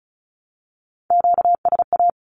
Morse code, heard as a keyed tone, about 34 wpm, 700 Hz. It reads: QSA
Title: Q-Codes 35wpm